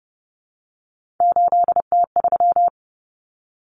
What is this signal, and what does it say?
Morse code, heard as a keyed tone, about 30 wpm, 700 Hz. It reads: 8T3
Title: Common Cut Numbers - 3-Digit 30wpm